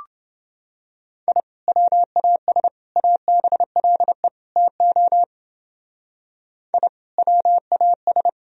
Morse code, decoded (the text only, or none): I WAS ABLE TO I WAS